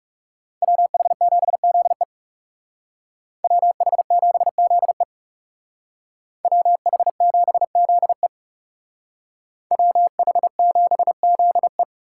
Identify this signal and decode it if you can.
Morse code, heard as a keyed tone, about 36 wpm, 700 Hz. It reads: WH7ZE WH7ZE WH7ZE WH7ZE
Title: Speed Racing - USA 2x2 Call Signs 30wpm